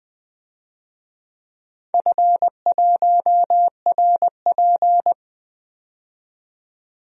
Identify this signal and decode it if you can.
Morse code, heard as a keyed tone, about 20 wpm, 700 Hz. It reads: F1RP
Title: DXCC-100 Call Signs - 4 Characters - 20wpm